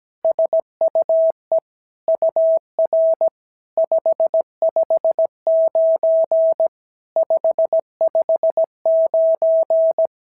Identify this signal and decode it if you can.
Morse code, heard as a keyed tone, about 17 wpm, 650 Hz. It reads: SUE UR 559 559
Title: SOTA QSO Practice 17wpm